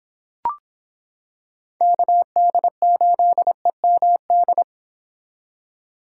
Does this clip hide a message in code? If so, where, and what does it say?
Yes: Morse code from 1.8 to 4.6 s, heard as a keyed tone, about 26 wpm, 700 Hz. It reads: KD8EMD